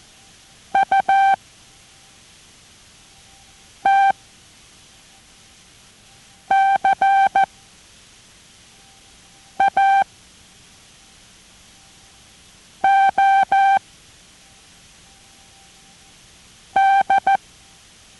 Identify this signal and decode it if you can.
Morse code, heard as a keyed tone, about 14 wpm, spaced as for 4 wpm, 780 Hz. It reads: U T CA O D